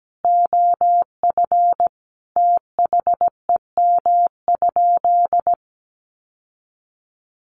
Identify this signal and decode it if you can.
Morse code, heard as a keyed tone, about 17 wpm, 700 Hz. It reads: OF THEM?
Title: Sentences from Top 600 Words 17wpm